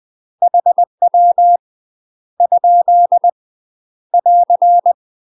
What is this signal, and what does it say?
Morse code, heard as a keyed tone, about 20 wpm, 700 Hz. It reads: HW ? <AR>
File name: UK MASTER QSOs TRAINING / 20WPM / UK MASTER QSOs 0017